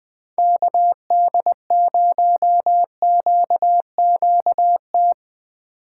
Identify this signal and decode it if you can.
Morse code, heard as a keyed tone, about 22 wpm, 700 Hz. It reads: KD0QQT
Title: Speed Racing - USA 2x3 Call Signs 20wpm